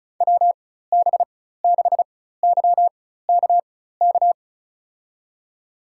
Morse code, heard as a keyed tone, about 36 wpm, spaced as for 16 wpm, 700 Hz. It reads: WB6YKK
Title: USA Call Signs - 6 Characters - 4x Character Spacing - 35wpm